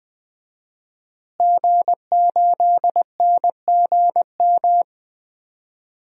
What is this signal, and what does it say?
Morse code, heard as a keyed tone, about 20 wpm, 700 Hz. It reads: G8NGM
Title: DXCC-100 Call Signs - 5 Characters - Repeated 3x - 20wpm